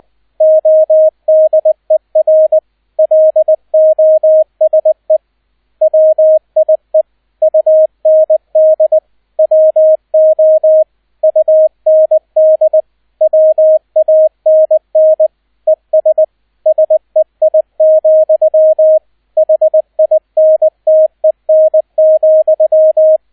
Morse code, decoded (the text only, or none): ODER LOSE WIE UND WO UND WANN ES SEI, HINTEN,